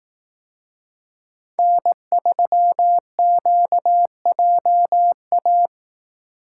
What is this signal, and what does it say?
Morse code, heard as a keyed tone, about 18 wpm, 700 Hz. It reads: N3QJA
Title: Speed Racing - USA 1x3 Call Signs 15wpm